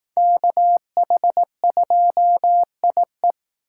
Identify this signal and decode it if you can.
Morse code, heard as a keyed tone, about 18 wpm, 700 Hz. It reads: KH2IE
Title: Speed Racing - USA 2x2 Call Signs 15wpm